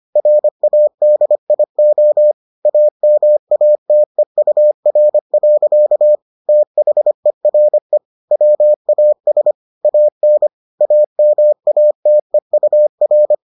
Morse code, heard as a keyed tone, about 25 wpm, 600 Hz. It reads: RADIO AMATEUR. THERE WAS AN AMATEUR